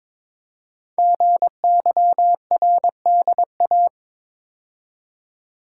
Morse code, heard as a keyed tone, about 22 wpm, 700 Hz. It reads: GYRDA
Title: Code Groups - 5 Letters - 22wpm